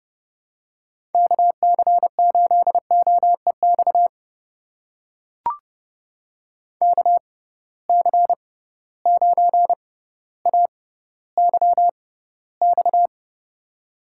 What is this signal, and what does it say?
Morse code, heard as a keyed tone, about 30 wpm, spaced as for 15 wpm, 700 Hz. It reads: KC8OEX KC9AYX
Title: USA Call Signs - 6 Characters - 6x Character Spacing - 30wpm